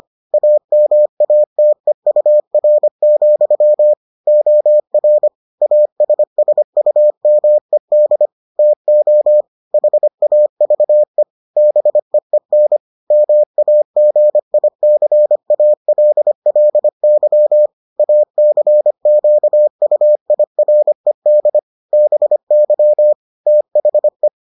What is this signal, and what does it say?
Morse code, heard as a keyed tone, about 25 wpm, 600 Hz. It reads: AMATEUR, OR ASSUMED TO HAVE BEEN MAGICALLY ACQUIRED BY THE